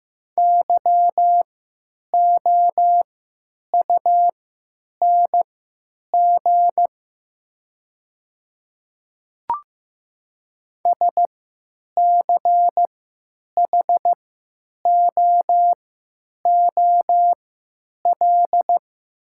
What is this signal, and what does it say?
Morse code, heard as a keyed tone, about 15 wpm, spaced as for 9 wpm, 700 Hz. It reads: YOUNG SCHOOL
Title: 3x Character Spacing - Top 500 Words - 15wpm